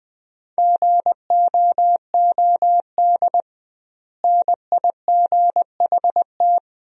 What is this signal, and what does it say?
Morse code, heard as a keyed tone, about 20 wpm, 700 Hz. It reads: GOOD NIGHT
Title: 2x Word-Spacing - Sentences from Top 300 Words - 20wpm